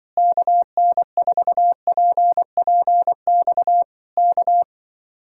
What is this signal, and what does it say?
Morse code, heard as a keyed tone, about 24 wpm, 700 Hz. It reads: KN4PPX K
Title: Speed-Racing Common QSO Phrases - 20wpm